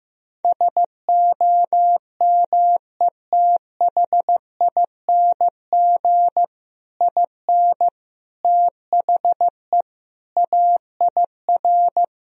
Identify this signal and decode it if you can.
Morse code, heard as a keyed tone, about 15 wpm, 700 Hz. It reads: SOMETHING IN THE AIR
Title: Sentences from Top 1000 Words - Review Part 1 - 15wpm